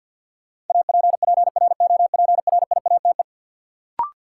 Morse code, heard as a keyed tone, about 52 wpm, 700 Hz. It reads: APPROPRIATE
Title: Sentences from Top 900 Words - New Word Drill - 50wpm